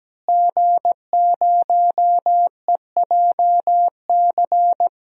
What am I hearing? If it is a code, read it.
Morse code, heard as a keyed tone, about 17 wpm, 700 Hz. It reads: G0EJC
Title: DXCC-100 Call Signs - 5 Characters - Repeated 3x - 17wpm